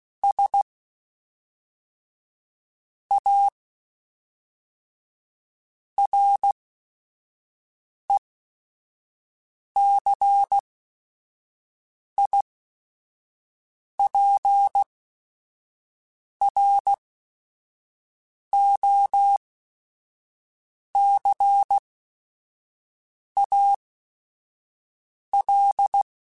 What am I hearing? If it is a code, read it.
Morse code, heard as a keyed tone, about 16 wpm, spaced as for 5 wpm, 800 Hz. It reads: S A RECIPROCAL
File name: Technical Subjects / Technical Subjects-mp3s / XC @5